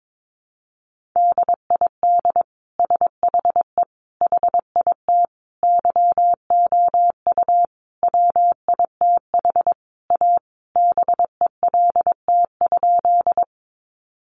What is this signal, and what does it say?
Morse code, heard as a keyed tone, about 22 wpm, 700 Hz. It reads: DID SHE HIT YOU WITH A BELT?
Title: Sentences from Top 1400 Words 22wpm